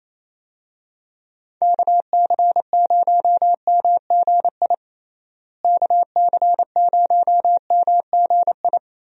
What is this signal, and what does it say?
Morse code, heard as a keyed tone, about 28 wpm, 700 Hz. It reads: KC0MGI KC0MGI
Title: USA Call Signs - 6 Characters - Repeated 3x - 28wpm